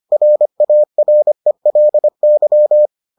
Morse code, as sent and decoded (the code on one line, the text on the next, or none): .-. .- .-. .-.. -.--
RARELY